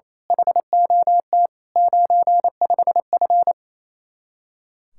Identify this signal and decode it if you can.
Morse code, heard as a keyed tone, about 28 wpm, 700 Hz. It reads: HOT 95F
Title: Speed-Racing Common QSO Phrases - 28wpm